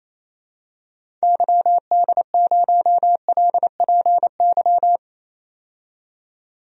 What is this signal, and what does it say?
Morse code, heard as a keyed tone, about 28 wpm, 700 Hz. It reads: YD0LPY
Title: DXCC-100 Call Signs - 6 Characters - 28wpm